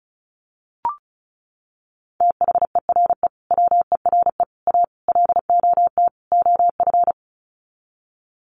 Morse code, heard as a keyed tone, about 34 wpm, 700 Hz. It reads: THERE WERE A LOT OF
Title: Sets of 5 Words 35wpm